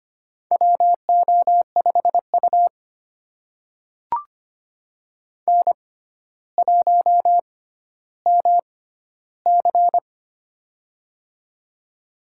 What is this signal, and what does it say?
Morse code, heard as a keyed tone, about 25 wpm, spaced as for 13 wpm, 700 Hz. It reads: WO5U N1MC
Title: USA Call Signs - 4 Characters - 6x Character Spacing - 25wpm